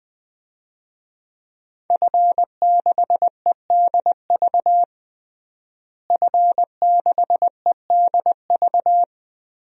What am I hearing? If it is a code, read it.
Morse code, heard as a keyed tone, about 21 wpm, 700 Hz. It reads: F6EDV F6EDV
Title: DXCC-100 Call Signs - 5 Characters - Repeated 2x - 20wpm